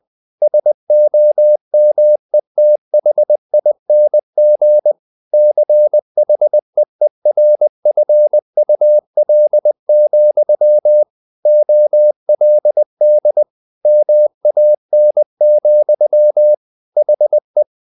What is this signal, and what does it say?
Morse code, heard as a keyed tone, about 20 wpm, 600 Hz. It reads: SOMETHING CHEERFUL, OLD MAN, HE
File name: the old man and the sea part 2 20wpm